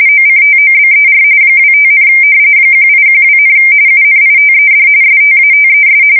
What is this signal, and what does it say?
RTTY, about 45.45 baud. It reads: OCTOBER 25, 1812, UNITED STATES
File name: fsk170 / usnavy2